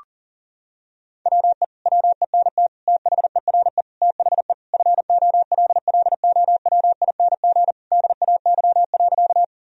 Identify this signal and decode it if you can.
Morse code, heard as a keyed tone, about 40 wpm, 700 Hz. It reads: WE WENT THERE THE FOLLOWING DAY.
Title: Sentences from Top 1000 Words - Review Part 4 - 40wpm